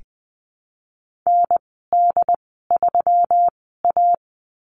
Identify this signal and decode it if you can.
Morse code, heard as a keyed tone, about 20 wpm, spaced as for 15 wpm, 700 Hz. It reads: ND3A
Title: 179 Lesson 29g - Slash - Call Signs 20wpm